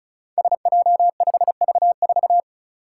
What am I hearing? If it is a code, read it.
Morse code, heard as a keyed tone, about 34 wpm, 700 Hz. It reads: SJ5V4
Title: Five Challenging Letter-Number Combinations 35wpm